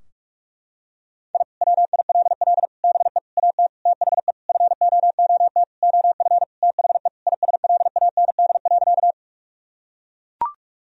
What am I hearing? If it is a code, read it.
Morse code, heard as a keyed tone, about 41 wpm, 700 Hz. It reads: I WILL BE AT THE FOOT OF THE ISLAND.
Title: Sentences from Top 1000 Words - Review Part 3 - 45wpm